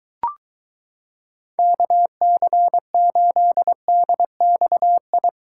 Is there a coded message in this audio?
Yes: Morse code from 1.6 to 5.3 s, heard as a keyed tone, about 23 wpm, 700 Hz. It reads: KC8DXI